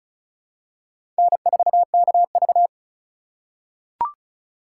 Morse code, heard as a keyed tone, about 36 wpm, 690 Hz. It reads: N4KV
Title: USA Call Signs - 4 Characters - 2x Character Spacing - 35wpm